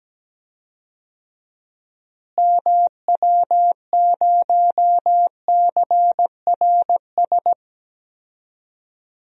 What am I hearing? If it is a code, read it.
Morse code, heard as a keyed tone, about 17 wpm, 700 Hz. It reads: MW0CRS
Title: DXCC-100 Call Signs - 6 Characters - Repeated 2x - 17wpm